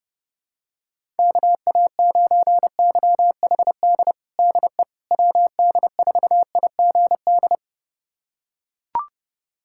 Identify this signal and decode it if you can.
Morse code, heard as a keyed tone, about 32 wpm, 700 Hz. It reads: KA9YHD DE WD4IGD
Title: USA Callsign DE Callsign 30wpm